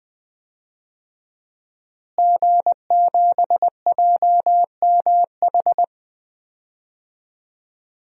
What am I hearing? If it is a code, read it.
Morse code, heard as a keyed tone, about 20 wpm, 700 Hz. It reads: G7JMH
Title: DXCC-100 Call Signs - 5 Characters - Repeated 3x - 20wpm